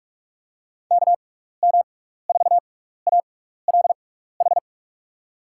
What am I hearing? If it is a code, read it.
Morse code, heard as a keyed tone, about 41 wpm, spaced as for 15 wpm, 700 Hz. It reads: KM4ALH
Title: USA Call Signs - 6 Characters - 6x Character Spacing - 45wpm